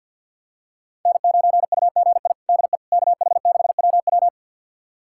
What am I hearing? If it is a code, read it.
Morse code, heard as a keyed tone, about 52 wpm, 700 Hz. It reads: N9UGI DE KH6WW
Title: USA Callsign DE Callsign - 5 Characters - 50wpm